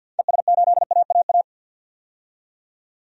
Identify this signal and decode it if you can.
Morse code, heard as a keyed tone, about 50 wpm, 700 Hz. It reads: ES8AAA
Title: DXCC-100 Call Signs - 6 Characters - 50wpm